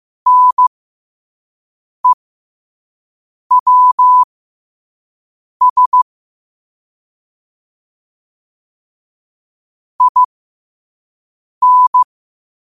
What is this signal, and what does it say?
Morse code, heard as a keyed tone, about 15 wpm, spaced as for 5 wpm, 1000 Hz. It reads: NEWS IN
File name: qotdfiles / 5wpm